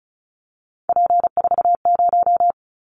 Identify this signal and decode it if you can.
Morse code, heard as a keyed tone, about 34 wpm, 700 Hz. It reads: P40
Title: Callsign Prefixes for Top 100 DXCC - 35wpm